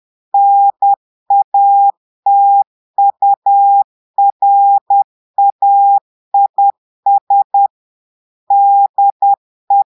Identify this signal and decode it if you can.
Morse code, heard as a keyed tone, about 10 wpm, 800 Hz. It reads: NATURAIS DE
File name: portugais texte en clair 2 vitesse 10 mots minutes